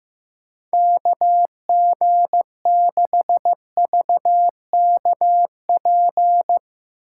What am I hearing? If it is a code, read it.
Morse code, heard as a keyed tone, about 15 wpm, 700 Hz. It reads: KG6VKP
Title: USA Call Signs - 6 Characters - 3x Character Spacing - 15wpm